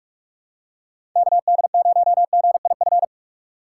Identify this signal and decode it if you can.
Morse code, heard as a keyed tone, about 44 wpm, 700 Hz. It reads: KD0GIF
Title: USA Call Signs - 6x Character Spacing - 45wpm